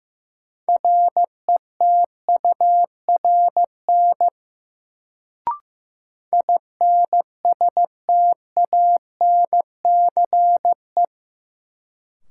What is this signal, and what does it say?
Morse code, heard as a keyed tone, about 15 wpm, 700 Hz. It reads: RETURN INSTANCE
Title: Sentences from Top 800 Words - New Word Drill - 15wpm